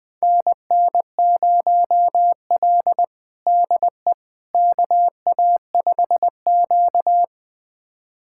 Morse code, heard as a keyed tone, about 20 wpm, 700 Hz. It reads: NN0L DE KA5Q
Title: USA Callsign DE Callsign - 4 Characters - 20wpm